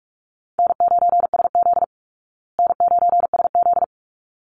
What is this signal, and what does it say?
Morse code, heard as a keyed tone, about 44 wpm, 700 Hz. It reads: N9SZ N9SZ
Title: Easy Call Signs Repeated 2x - 45wpm